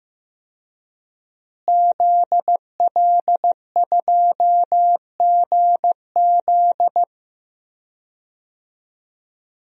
Morse code, heard as a keyed tone, about 15 wpm, 700 Hz. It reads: ZL2GZ
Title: DXCC-100 Call Signs - 5 Characters - Repeated 3x - 15wpm